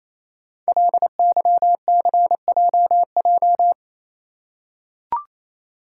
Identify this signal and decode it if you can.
Morse code, heard as a keyed tone, about 28 wpm, 700 Hz. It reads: LYCJJ
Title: Code Groups - 5 Letter:Numbers - 28wpm